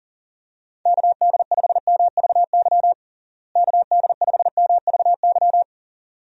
Morse code, heard as a keyed tone, about 37 wpm, 700 Hz. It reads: KD5MVY KD5MVY
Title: USA Call Signs - 6 Characters - Repeated 2x - 40wpm